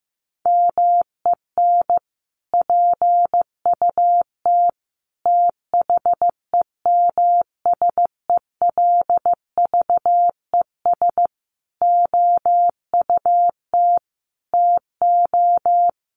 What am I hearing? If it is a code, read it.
Morse code, heard as a keyed tone, about 15 wpm, 700 Hz. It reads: MEN PUT THEMSELVES OUT TO
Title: Sentences from Top 1400 Words 15wpm